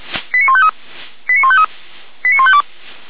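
Five-tone selective call (ZVEI), heard as quick runs of five tones, tones of about 70 ms. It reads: 89153 89153 89152
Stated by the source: ZVEI selective call received by SDR.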